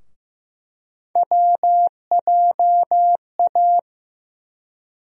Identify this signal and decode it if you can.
Morse code, heard as a keyed tone, about 15 wpm, 700 Hz. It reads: WJA
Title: USA Call Sign Postfixes - 3 Letters 15wpm